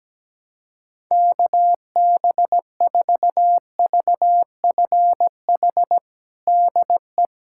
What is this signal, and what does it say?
Morse code, heard as a keyed tone, about 17 wpm, 700 Hz. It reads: KB4VFH DE
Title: USA Callsign DE Callsign 17wpm